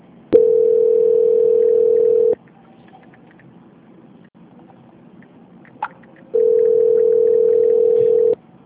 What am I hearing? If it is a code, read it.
Ringback tone (North America), 440 + 480 Hz.